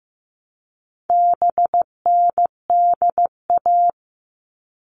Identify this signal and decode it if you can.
Morse code, heard as a keyed tone, about 15 wpm, 700 Hz. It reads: BNDA
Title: All Cut Numbers - 4-Digit 15wpm